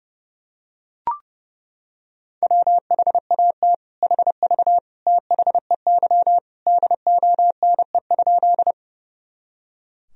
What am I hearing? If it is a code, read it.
Morse code, heard as a keyed tone, about 30 wpm, 700 Hz. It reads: WHAT HV THEY DONE?
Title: Sentences from Top 500 Words and Abbreviations 30wpm